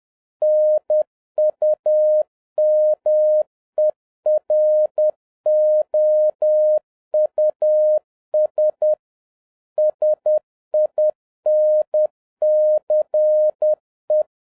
Morse code, heard as a keyed tone, about 10 wpm, 610 Hz. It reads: NUMEROUS SINCE